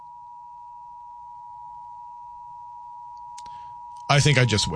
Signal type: Emergency Alert System (SAME) tones